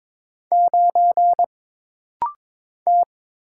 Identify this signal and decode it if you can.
Morse code, heard as a keyed tone, about 22 wpm, 700 Hz. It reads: 9 T
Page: This rapid-fire timing is geared to push your ICR (Instant Character Recognition) skills.